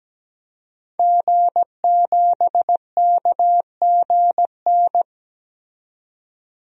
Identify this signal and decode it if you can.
Morse code, heard as a keyed tone, about 17 wpm, 700 Hz. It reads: G7KGN
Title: DXCC-100 Call Signs - 5 Characters - 17wpm